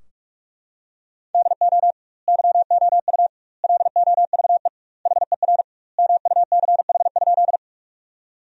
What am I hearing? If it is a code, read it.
Morse code, heard as a keyed tone, about 42 wpm, 700 Hz. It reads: DO YOU LOVE HER MUCH?
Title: 2x Word-Spacing - Sentences from Top 400 Words - 45wpm